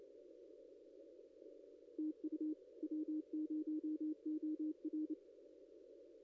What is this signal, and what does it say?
Morse code, heard as a keyed tone, about 26 wpm, 310 Hz. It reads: TU W0OR
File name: contests / ss-cw-k5tr-op-n5rz -0900